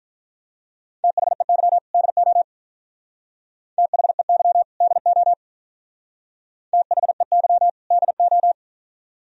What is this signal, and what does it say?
Morse code, heard as a keyed tone, about 43 wpm, 700 Hz. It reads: THEY DO THEY DO THEY DO
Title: Speed-Racing - Sets of 2 Words 35wpm